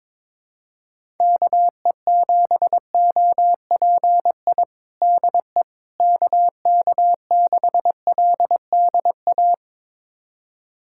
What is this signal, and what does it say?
Morse code, heard as a keyed tone, about 22 wpm, 700 Hz. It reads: KE7OPI DE KK6LDA
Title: USA Callsign DE Callsign - 6 Characters - 22wpm